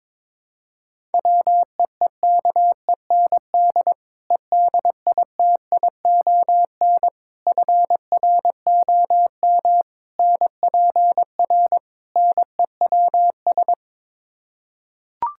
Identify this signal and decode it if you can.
Morse code, heard as a keyed tone, about 22 wpm, 700 Hz. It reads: WEEKEND EDITION FROM NPR NEWS
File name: practice / 22wpm / Build A Phrase - 5-Word Phrases - 22wpm